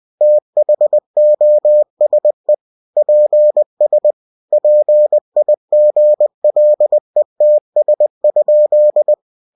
Morse code, heard as a keyed tone, about 20 wpm, 600 Hz. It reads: THOSE PS PIGLETS?
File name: cw-books / winnie-the-pooh / winnie-20wpm-page-0098